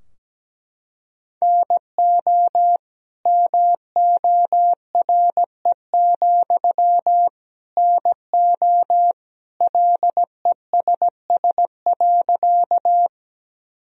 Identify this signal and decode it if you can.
Morse code, heard as a keyed tone, about 17 wpm, 700 Hz. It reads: NO MORE, NO LESS.
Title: Sentences from Top 1000 Words - Review Part 2 - 17wpm